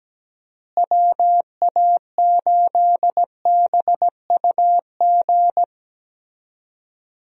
Morse code, heard as a keyed tone, about 17 wpm, 700 Hz. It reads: WA8BUG
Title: USA Call Signs - 6 Characters - Repeated 2x - 17wpm